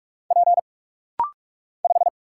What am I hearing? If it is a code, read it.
Morse code, heard as a keyed tone, about 44 wpm, 700 Hz. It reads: P 5
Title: Single Letter-Number Alt - Rapid-Fire 45wpm